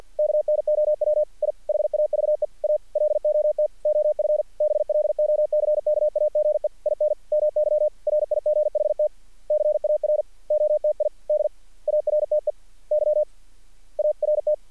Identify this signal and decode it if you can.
Morse code, heard as a keyed tone, about 50 wpm, spaced as for 44 wpm, 600 Hz. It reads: KNOW I HAVE A LOT OF BLOCKAGE IN MY RIGHT CAROTID ARTEY ART
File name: archived / CRSnet